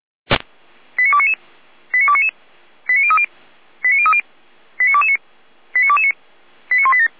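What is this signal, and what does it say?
Five-tone selective call (ZVEI), heard as quick runs of five tones, tones of about 70 ms. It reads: 89299 89299 89030 89030 89229 89229 89187
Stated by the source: ZVEI five-tone call received by SDR, from the feuerwehr-orlamuende.org server (fire brigade Orlamünde).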